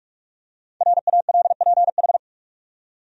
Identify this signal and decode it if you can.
Morse code, heard as a keyed tone, about 46 wpm, 700 Hz. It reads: RALPH
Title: USA Common Names - 45wpm